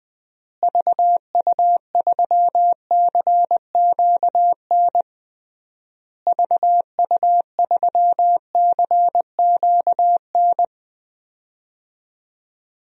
Morse code, heard as a keyed tone, about 20 wpm, 700 Hz. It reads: VU3CQN VU3CQN
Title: DXCC-100 Call Signs - 6 Characters - Repeated 3x - 20wpm